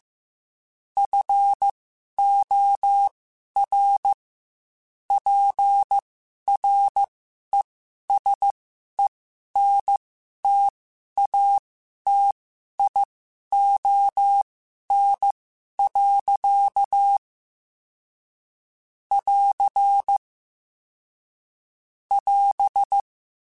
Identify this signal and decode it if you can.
Morse code, heard as a keyed tone, about 15 wpm, spaced as for 11 wpm, 800 Hz. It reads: FOR PRESENTATION. <AR> <AS>
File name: Quotes / Quotes-mp3s / Persperation @13